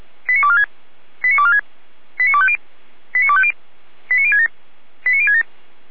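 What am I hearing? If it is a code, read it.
Five-tone selective call (ZVEI), heard as quick runs of five tones, tones of about 80 ms. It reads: 89246 89246 89250 89250 89067 89067
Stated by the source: ZVEI five-tone call 89108 received by SDR, from audio.f85360.feuerwehr-orlamuende.org fire brigade (Feuerwehr Orlamünde) archive.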